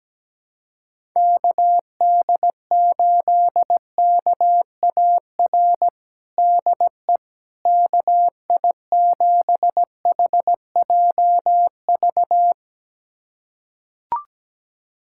Morse code, heard as a keyed tone, about 17 wpm, 700 Hz. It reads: KD8KAR DE KI7HJV
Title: USA Callsign DE Callsign - 6 Characters - 17wpm